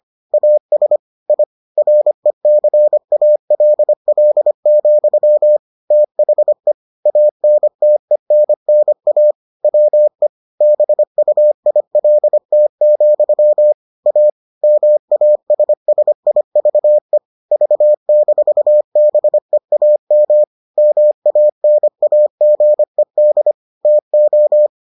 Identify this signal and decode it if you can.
Morse code, heard as a keyed tone, about 25 wpm, 600 Hz. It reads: AS I RECALL, THE ANTENNA WE BUILT, A MASSIVE V-BEAM MANAGED TO